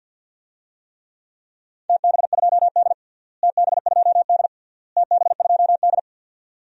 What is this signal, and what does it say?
Morse code, heard as a keyed tone, about 52 wpm, 700 Hz. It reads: TB2D TB2D TB2D
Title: DXCC-100 Call Signs - 4 Characters - Repeated 3x - 50wpm